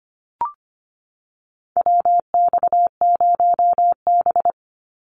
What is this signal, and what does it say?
Morse code, heard as a keyed tone, about 25 wpm, 700 Hz. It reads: WX0B
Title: Call Signs Repeated 2x - 25wpm